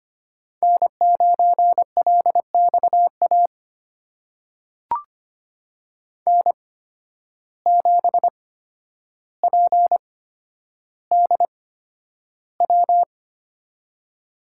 Morse code, heard as a keyed tone, about 25 wpm, spaced as for 11 wpm, 700 Hz. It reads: N9LXA N7PDW
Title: USA Call Signs - 5 Characters - 8x Character Spacing - 25wpm